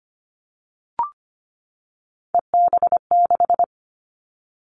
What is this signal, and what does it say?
Morse code, heard as a keyed tone, about 25 wpm, 700 Hz. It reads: EB6
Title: Callsign Prefixes for Top 50 DXCC - 25wpm